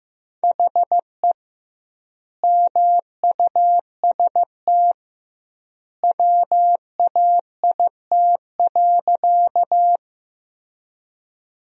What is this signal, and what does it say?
Morse code, heard as a keyed tone, about 15 wpm, 700 Hz. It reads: HE MUST WAIT.
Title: 2x Word-Spacing - Sentences from Top 500 Words - 15wpm